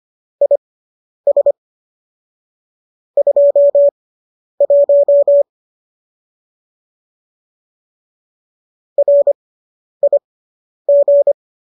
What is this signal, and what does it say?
Morse code, heard as a keyed tone, about 25 wpm, spaced as for 10 wpm, 570 Hz. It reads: IS 21 RIG